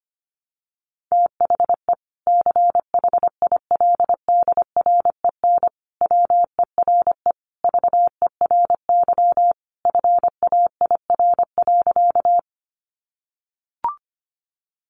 Morse code, heard as a keyed tone, about 25 wpm, 700 Hz. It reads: THE CHILDREN WERE VERY FAIR.
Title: Sentences from Top 1300 Words 25wpm